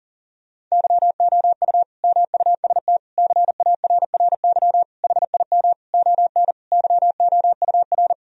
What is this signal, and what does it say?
Morse code, heard as a keyed tone, about 37 wpm, 700 Hz. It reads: YOU MUST CARRY HIM ON YOUR
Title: Sentences from Top 1000 Words - Review Part 1 - 40wpm